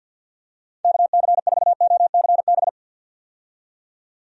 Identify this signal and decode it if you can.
Morse code, heard as a keyed tone, about 51 wpm, 700 Hz. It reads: KC4OCB
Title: USA Call Signs - 6 Characters - 50wpm